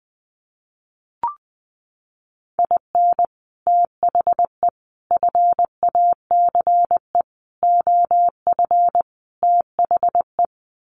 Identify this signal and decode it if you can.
Morse code, heard as a keyed tone, about 20 wpm, 700 Hz. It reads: IN THE FACE OF THE
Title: Sets of 5 Words 20wpm